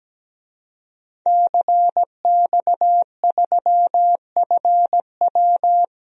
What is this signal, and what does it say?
Morse code, heard as a keyed tone, about 17 wpm, 700 Hz. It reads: CX3FW
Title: DXCC-100 Call Signs - 5 Characters - 17wpm